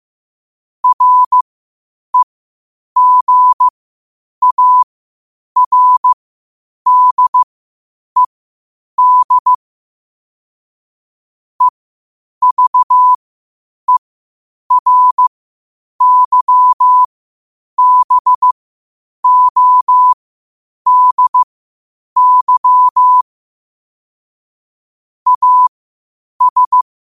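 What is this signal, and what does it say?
Morse code, heard as a keyed tone, about 15 wpm, spaced as for 8 wpm, 1000 Hz. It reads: REGARDED EVERYBODY AS
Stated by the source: Morse code at 8 wpm.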